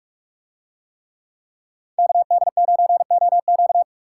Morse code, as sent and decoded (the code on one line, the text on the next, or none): -.- -.. ----. --- --.-
KD9OQ